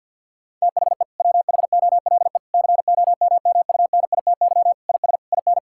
Morse code, heard as a keyed tone, about 50 wpm, 700 Hz. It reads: THE WHOLE COMMUNITY IS IN
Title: Sentences from Top 1600 Words 50wpm